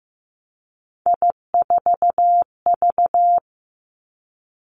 Morse code, heard as a keyed tone, about 15 wpm, 700 Hz. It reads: I4V
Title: Three Challenging Letter-Number Combinations 15wpm